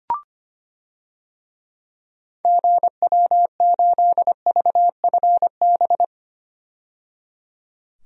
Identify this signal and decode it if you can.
Morse code, heard as a keyed tone, about 25 wpm, 700 Hz. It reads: GW8VFB
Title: DXCC-100 Call Signs - 6 Characters - 25wpm